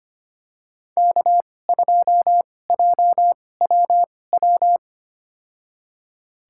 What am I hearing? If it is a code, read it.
Morse code, heard as a keyed tone, about 25 wpm, spaced as for 18 wpm, 700 Hz. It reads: K2JWW
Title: USA Call Signs - 2x Character Spacing - 25wpm